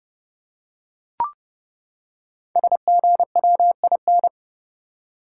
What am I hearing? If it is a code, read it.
Morse code, heard as a keyed tone, about 30 wpm, 700 Hz. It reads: SGWIN
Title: Code Groups - 5 Letters - 30wpm